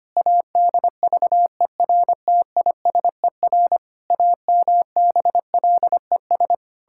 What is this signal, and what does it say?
Morse code, heard as a keyed tone, about 25 wpm, 700 Hz. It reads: ADVERTISER AMBLES
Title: Pangram Challenge v2 - 25wpm